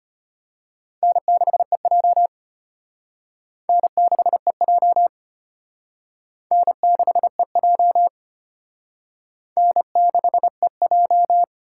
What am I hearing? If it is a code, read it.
Morse code, heard as a keyed tone, about 33 wpm, 700 Hz. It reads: N6EJ N6EJ N6EJ N6EJ